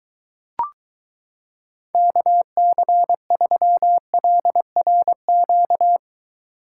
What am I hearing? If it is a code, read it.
Morse code, heard as a keyed tone, about 23 wpm, 700 Hz. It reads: KC3LRQ